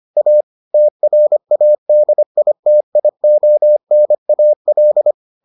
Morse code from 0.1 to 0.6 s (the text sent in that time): A